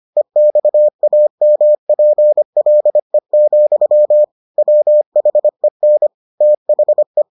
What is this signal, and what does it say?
Morse code, heard as a keyed tone, about 25 wpm, 600 Hz. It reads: EXAMPLE, WHEN THE